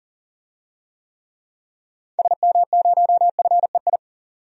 Morse code, heard as a keyed tone, about 40 wpm, 700 Hz. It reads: SM0FEI
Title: DXCC-100 Call Signs - 6 Characters - 40wpm